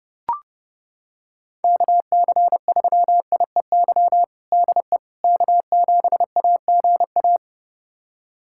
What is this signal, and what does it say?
Morse code, heard as a keyed tone, about 30 wpm, 700 Hz. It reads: KC3IEY DE K7AGA
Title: USA Callsign DE Callsign 30wpm